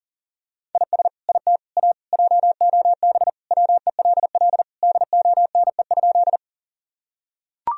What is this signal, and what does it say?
Morse code, heard as a keyed tone, about 40 wpm, 700 Hz. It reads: IS IT A JOB WELL DONE?
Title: Sentences from Top 1000 Words - Review Part 3 - 40wpm